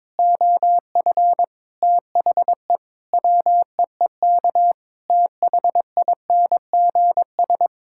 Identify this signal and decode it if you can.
Morse code, heard as a keyed tone, about 22 wpm, 700 Hz. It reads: OF THE WEEK THINGS
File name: practice / 22wpm / Sentences with 4-Word Phrases - 22wpm